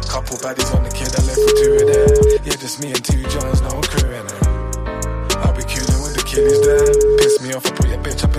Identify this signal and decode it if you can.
Ringback tone (Europe, 425 Hz), with speech.